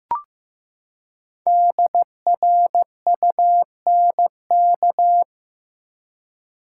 Morse code, heard as a keyed tone, about 15 wpm, 700 Hz. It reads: DRUNK
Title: Five Letter Words - 15wpm